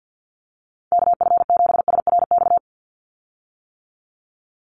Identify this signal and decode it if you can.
Morse code, heard as a keyed tone, about 50 wpm, 700 Hz. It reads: KF7SRX
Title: USA Call Signs 50wpm